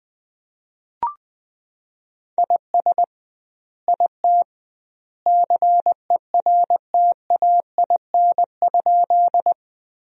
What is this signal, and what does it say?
Morse code, heard as a keyed tone, about 21 wpm, 700 Hz. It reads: IS IT CERTAIN?
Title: 2x Word-Spacing - Sentences from Top 500 Words - 20wpm